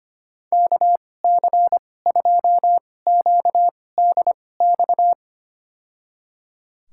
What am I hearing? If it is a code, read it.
Morse code, heard as a keyed tone, about 25 wpm, spaced as for 18 wpm, 700 Hz. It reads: KC2QDX
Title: USA Call Signs - 6 Characters - 2x Character Spacing - 25wpm